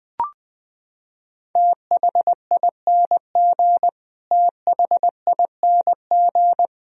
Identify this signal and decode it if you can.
Morse code, heard as a keyed tone, about 20 wpm, 700 Hz. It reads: THING THING
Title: Top 100 Words Repeated 20wpm